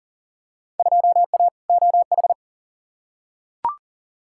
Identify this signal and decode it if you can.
Morse code, heard as a keyed tone, about 38 wpm, 700 Hz. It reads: 2A OH
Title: Field Day Exchanges - 40wpm